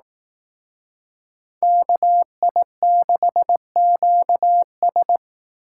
Morse code, heard as a keyed tone, about 18 wpm, 700 Hz. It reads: KI6QS
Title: Speed Racing - USA 2x2 Call Signs 15wpm